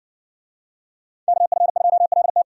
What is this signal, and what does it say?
Morse code, heard as a keyed tone, about 60 wpm, 700 Hz. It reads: KF2LT